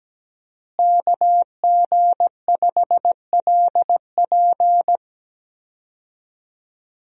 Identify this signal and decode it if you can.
Morse code, heard as a keyed tone, about 17 wpm, 700 Hz. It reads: KG5LP
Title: USA Call Signs - 5 Characters - Repeated 3x - 17wpm